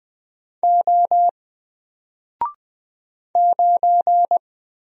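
Morse code, heard as a keyed tone, about 20 wpm, 700 Hz. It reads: O 9
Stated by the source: Single letter and number Morse drill at 20 wpm, from N4LCD.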